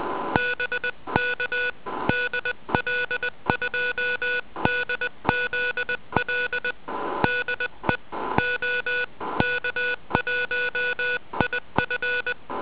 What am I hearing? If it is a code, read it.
Morse code, heard as a keyed tone, about 20 wpm, 1470 Hz. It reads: BK DL2DZL DE OK1IF